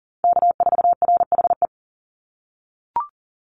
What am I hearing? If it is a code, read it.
Morse code, heard as a keyed tone, about 38 wpm, 700 Hz. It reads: K4RHE
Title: USA Call Signs 40wpm